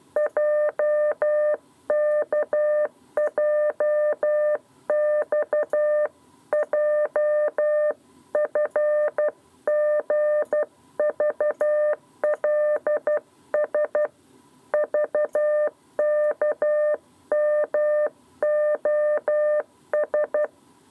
Morse code, heard as a keyed tone, about 12 wpm, 600 Hz. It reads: JKJXJ FGVLS VKMOS
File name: 7b lezione cw 2